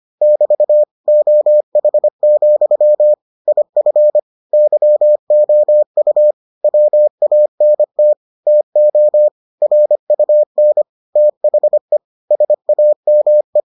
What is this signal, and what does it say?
Morse code, heard as a keyed tone, about 25 wpm, 600 Hz. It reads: <BT> OH, IF YOU WANT TO RUN THE SAME